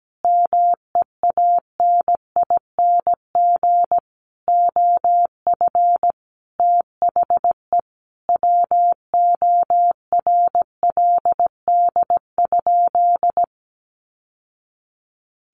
Morse code, heard as a keyed tone, about 17 wpm, 700 Hz. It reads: MEANING OF THE WORLD?
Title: Sentences from Top 900 Words 17wpm